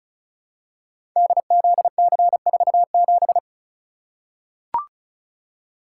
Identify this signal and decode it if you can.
Morse code, heard as a keyed tone, about 36 wpm, 700 Hz. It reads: DZC47
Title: Code Groups - 5 Letter:Numbers - 35wpm